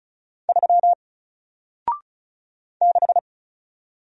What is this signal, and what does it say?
Morse code, heard as a keyed tone, about 35 wpm, 700 Hz. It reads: 3 6
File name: Single-Letter-Number-35wpm